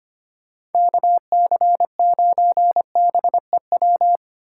Morse code, heard as a keyed tone, about 30 wpm, 700 Hz. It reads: KC9BEW
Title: Speed Racing - USA 2x3 Call Signs 25wpm